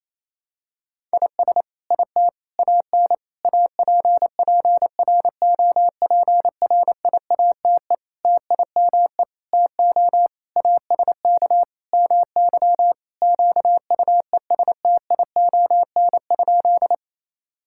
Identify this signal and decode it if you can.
Morse code, heard as a keyed tone, about 28 wpm, 700 Hz. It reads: IS IT AN APPROPRIATE TIME TO ASK MY QUESTION?
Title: Sentences from Top 1000 Words - Review Part 5 - 28wpm